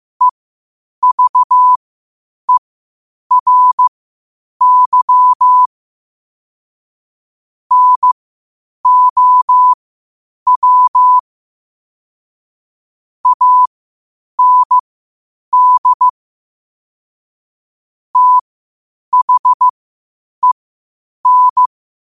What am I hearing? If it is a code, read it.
Morse code, heard as a keyed tone, about 15 wpm, spaced as for 8 wpm, 1000 Hz. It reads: EVERY NOW AND THEN